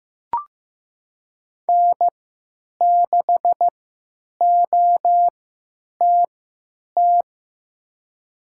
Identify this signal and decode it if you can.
Morse code, heard as a keyed tone, about 15 wpm, spaced as for 8 wpm, 700 Hz. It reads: N6OTT
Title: USA Call Signs - 5 Characters - 3x Character Spacing - 15wpm